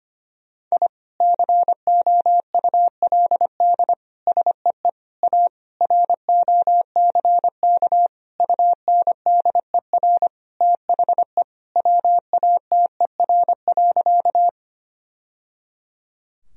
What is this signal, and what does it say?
Morse code, heard as a keyed tone, about 25 wpm, 700 Hz. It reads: I COULD SEE A ROCK UNDER THE WATER.
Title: Sentences from Top 1000 Words - Review Part 2 - 25wpm